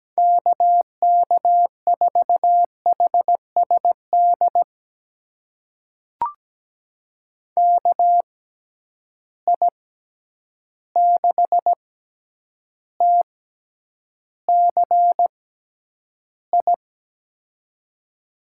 Morse code, heard as a keyed tone, about 17 wpm, spaced as for 9 wpm, 700 Hz. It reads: KK4HSD KI6TCI
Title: USA Call Signs - 6 Characters - 6x Character Spacing - 17wpm